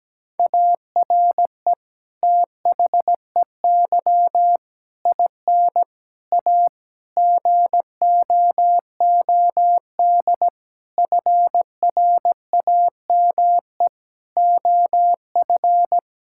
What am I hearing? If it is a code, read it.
Morse code, heard as a keyed tone, about 17 wpm, 700 Hz. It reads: ARE THEY IN A GOOD FRAME OF
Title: Sentences from Top 1000 Words - Review Part 6 - 17wpm